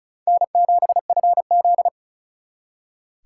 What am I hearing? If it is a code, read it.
Morse code, heard as a keyed tone, about 45 wpm, 700 Hz. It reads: N7FZ